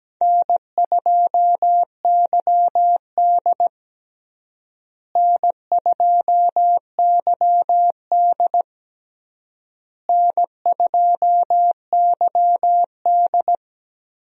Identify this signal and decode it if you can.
Morse code, heard as a keyed tone, about 17 wpm, 700 Hz. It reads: N2YD N2YD N2YD
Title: USA Call Signs - 4 Characters - Repeated 3x - 17wpm